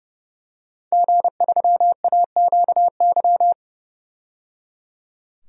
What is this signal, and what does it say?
Morse code, heard as a keyed tone, about 30 wpm, 700 Hz. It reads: G3AQY
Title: DXCC-100 Call Signs - 5 Characters - 30wpm